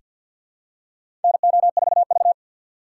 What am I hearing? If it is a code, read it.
Morse code, heard as a keyed tone, about 49 wpm, 700 Hz. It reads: NO4V
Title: USA Call Signs - 2x Character Spacing - 50wpm